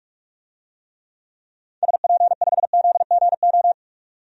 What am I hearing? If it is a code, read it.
Morse code, heard as a keyed tone, about 46 wpm, 700 Hz. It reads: SP5ZGO
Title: DXCC-100 Call Signs - 6 Characters - 45wpm